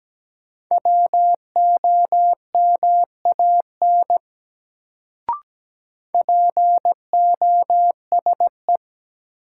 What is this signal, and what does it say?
Morse code, heard as a keyed tone, about 17 wpm, 700 Hz. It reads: WOMAN POSE
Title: Sentences from Top 1000 Words - Review Part 2 - New Word Drill - 17wpm